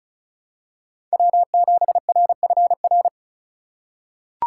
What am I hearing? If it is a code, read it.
Morse code, heard as a keyed tone, about 36 wpm, 700 Hz. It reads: W7RFR
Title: USA Call Signs - 5 Characters - 2x Character Spacing - 35wpm